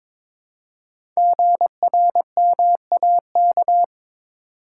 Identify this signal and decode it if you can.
Morse code, heard as a keyed tone, about 22 wpm, 700 Hz. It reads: GRMAK
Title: Code Groups - 5 Letters - 22wpm